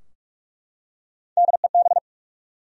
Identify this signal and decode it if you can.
Morse code, heard as a keyed tone, about 48 wpm, 700 Hz. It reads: DEB
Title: USA Common Names - Max 3 Letters - 45wpm